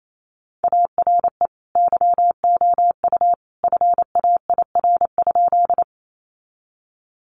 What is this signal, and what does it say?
Morse code, heard as a keyed tone, about 28 wpm, 700 Hz. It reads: ARE YOU FAIR?
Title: Sentences from Top 1300 Words 28wpm